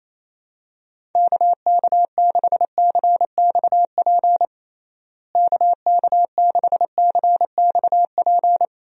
Morse code, heard as a keyed tone, about 28 wpm, 700 Hz. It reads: KK6CXP KK6CXP
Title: USA Call Signs - 6 Characters - Repeated 2x - 28wpm